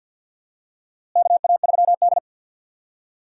Morse code, heard as a keyed tone, about 47 wpm, 700 Hz. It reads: KA3D
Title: USA Call Signs - 4 Characters - Repeated 2x - 50wpm